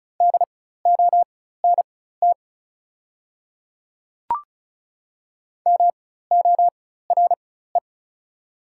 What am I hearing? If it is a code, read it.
Morse code, heard as a keyed tone, about 36 wpm, spaced as for 16 wpm, 690 Hz. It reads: DONT MORE
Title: 4x Character Spacing - Top 300 Words - 35wpm